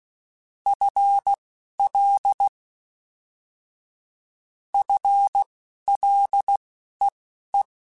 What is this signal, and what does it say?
Morse code, heard as a keyed tone, about 16 wpm, spaced as for 11 wpm, 800 Hz. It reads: FL FLEE